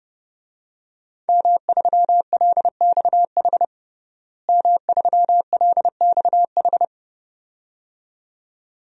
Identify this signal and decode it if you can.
Morse code, heard as a keyed tone, about 30 wpm, 700 Hz. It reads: M3LXH M3LXH
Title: DXCC-100 Call Signs - 5 Characters - Repeated 2x - 30wpm